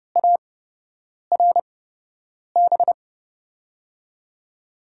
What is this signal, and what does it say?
Morse code, heard as a keyed tone, about 30 wpm, spaced as for 8 wpm, 700 Hz. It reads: ARB
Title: USA Call Signs - 6 Characters - 8x Character Spacing - 30wpm